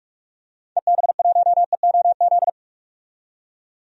Morse code, heard as a keyed tone, about 41 wpm, 700 Hz. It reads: EB1EOZ
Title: DXCC-100 Call Signs - 6 Characters - 45wpm